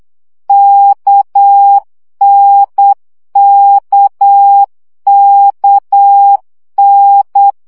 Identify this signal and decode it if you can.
Morse code, heard as a keyed tone, about 8 wpm, 800 Hz. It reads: KNKKN
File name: CW gyakorlas n d k 7 baud